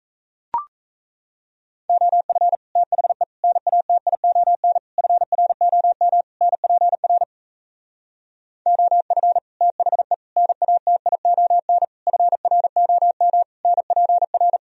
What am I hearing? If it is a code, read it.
Morse code, heard as a keyed tone, about 38 wpm, 700 Hz. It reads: OF THE NATION FROM NPR OF THE NATION FROM NPR